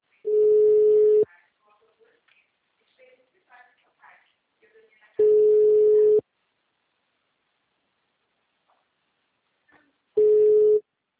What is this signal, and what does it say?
Ringback tone (Europe, 425 Hz), with speech.